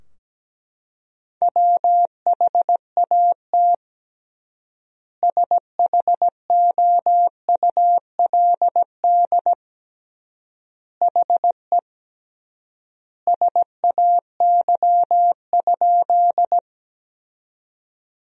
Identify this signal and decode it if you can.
Morse code, heard as a keyed tone, about 17 wpm, 700 Hz. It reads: WHAT SHOULD HE SAY?
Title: 3x Word-Spacing - Sentences from Top 300 Words - 17wpm